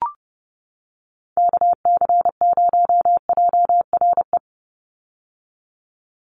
Morse code, heard as a keyed tone, about 30 wpm, 700 Hz. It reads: KC0JRE
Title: USA Call Signs 30wpm